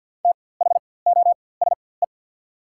Morse code, heard as a keyed tone, about 48 wpm, spaced as for 23 wpm, 690 Hz. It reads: THOSE
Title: 4x Character Spacing - Top 300 Words - 50wpm